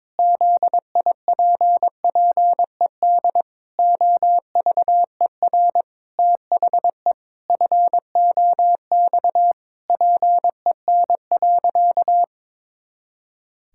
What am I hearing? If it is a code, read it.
Morse code, heard as a keyed tone, about 22 wpm, 700 Hz. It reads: ZIPPED OVER THE FOX PEN.
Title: Pangram Challenge v2 - 22wpm